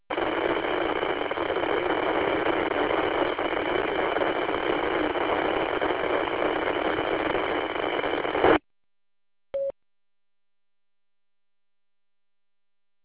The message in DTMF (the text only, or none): none